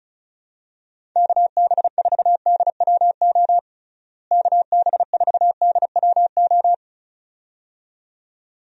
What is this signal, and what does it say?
Morse code, heard as a keyed tone, about 36 wpm, 700 Hz. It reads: KB4DWO KB4DWO
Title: USA Call Signs - 6 Characters - Repeated 2x - 35wpm